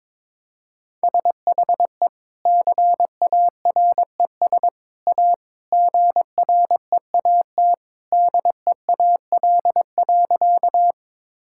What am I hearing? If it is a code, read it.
Morse code, heard as a keyed tone, about 22 wpm, 700 Hz. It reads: SHE CARES A GREAT DEAL.
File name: practice / 22wpm / Sentences with 3-Word Phrases - 22wpm